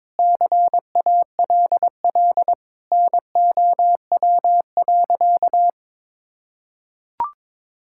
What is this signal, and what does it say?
Morse code, heard as a keyed tone, about 22 wpm, 700 Hz. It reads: CALL NOW.
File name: practice / 22wpm / Sentences with 4-Word Phrases - 22wpm